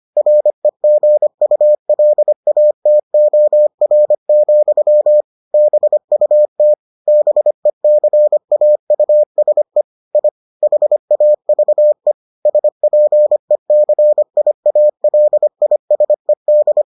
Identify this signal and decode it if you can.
Morse code, heard as a keyed tone, about 25 wpm, 600 Hz. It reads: REGULATOR, BUT BECAUSE I HAVE SPECIALISED